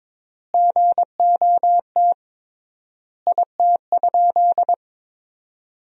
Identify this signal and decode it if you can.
Morse code, heard as a keyed tone, about 23 wpm, 700 Hz. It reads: GOT IT?
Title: 3x Word-Spacing - Sentences from Top 300 Words - 22wpm